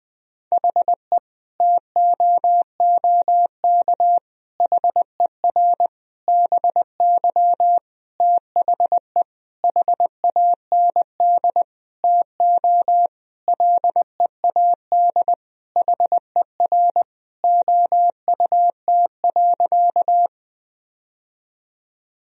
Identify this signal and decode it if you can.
Morse code, heard as a keyed tone, about 20 wpm, 700 Hz. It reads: HE TOOK HER BY THE HAND TO LEAD HER OUT.
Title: Sentences from Top 1000 Words - Review Part 2 - 20wpm